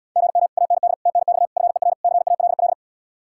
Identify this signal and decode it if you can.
Morse code, heard as a keyed tone, about 79 wpm, 700 Hz. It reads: GM AND TNX FER CALL